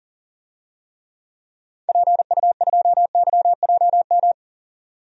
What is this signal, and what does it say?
Morse code, heard as a keyed tone, about 40 wpm, 700 Hz. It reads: PU2YJM
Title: DXCC-100 Call Signs - 6 Characters - 40wpm